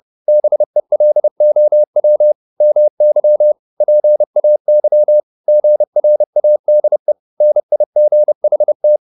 Morse code, heard as a keyed tone, about 30 wpm, 600 Hz. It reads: BELOW MY PAY GRADE NIGHT